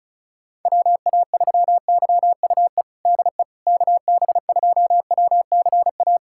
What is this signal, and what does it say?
Morse code, heard as a keyed tone, about 36 wpm, 700 Hz. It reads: WA3YUE DE KB2WCA
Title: USA Callsign DE Callsign 35wpm